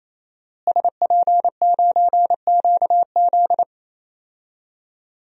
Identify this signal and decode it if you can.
Morse code, heard as a keyed tone, about 27 wpm, 700 Hz. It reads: SP9QZ
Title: DXCC-100 Call Signs - 5 Characters - Repeated 3x - 28wpm